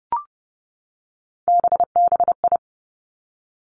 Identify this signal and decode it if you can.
Morse code, heard as a keyed tone, about 30 wpm, 700 Hz. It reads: BBI
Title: Three Challenging Letter-Number Combinations 30wpm